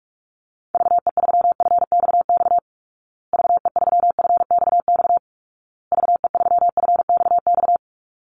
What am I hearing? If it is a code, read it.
Morse code, heard as a keyed tone, about 44 wpm, 700 Hz. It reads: VE3FXX VE3FXX VE3FXX
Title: Call Signs Repeated 2x - 45wpm